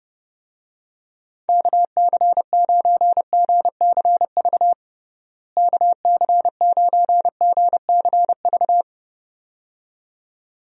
Morse code, heard as a keyed tone, about 30 wpm, 700 Hz. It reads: KC9GCV KC9GCV
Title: USA Call Signs - 6 Characters - Repeated 2x - 30wpm